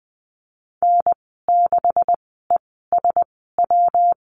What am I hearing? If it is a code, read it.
Morse code, heard as a keyed tone, about 20 wpm, spaced as for 15 wpm, 700 Hz. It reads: N6ESW
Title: 128 Lesson 21g - Number 6 - Call Signs 20wpm